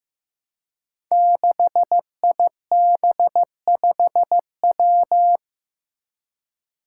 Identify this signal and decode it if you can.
Morse code, heard as a keyed tone, about 16 wpm, 700 Hz. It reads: 6IB5W
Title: Five Challenging Letter-Number Combinations 15wpm